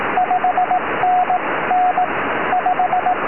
Morse code, heard as a keyed tone, about 18 wpm, spaced as for 15 wpm, 730 Hz, with radio noise. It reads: 5NN 5
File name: tropo / g4loh 2m